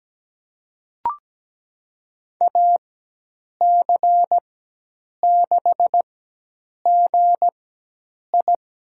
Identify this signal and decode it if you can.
Morse code, heard as a keyed tone, about 17 wpm, spaced as for 8 wpm, 700 Hz. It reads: AC6GI
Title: USA Call Signs - 5 Characters - 4x Character Spacing - 17wpm